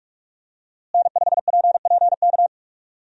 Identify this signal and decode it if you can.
Morse code, heard as a keyed tone, about 46 wpm, 700 Hz. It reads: N5PPK
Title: USA Call Signs - 5 Characters - Repeated 3x - 45wpm